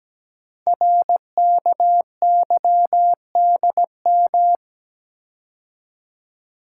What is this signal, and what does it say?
Morse code, heard as a keyed tone, about 17 wpm, 700 Hz. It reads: RKYDM
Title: Code Groups - 5 Letters - 17wpm